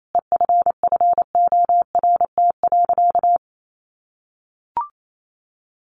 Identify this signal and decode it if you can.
Morse code, heard as a keyed tone, about 28 wpm, 700 Hz. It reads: EFFORT.
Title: Sentences from Top 1200 Words 28wpm